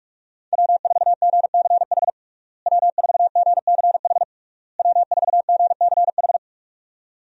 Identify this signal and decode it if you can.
Morse code, heard as a keyed tone, about 46 wpm, 700 Hz. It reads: W4GCH W4GCH W4GCH
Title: USA Call Signs - 5 Characters - Repeated 3x - 45wpm